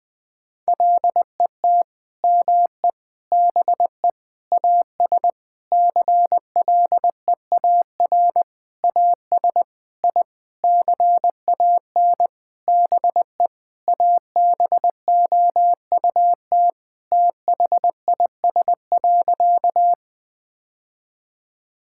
Morse code, heard as a keyed tone, about 20 wpm, 700 Hz. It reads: LET ME BE AS CLEAR AS I CAN BE ABOUT THIS.
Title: Sentences with 9-Word Phrases - 20wpm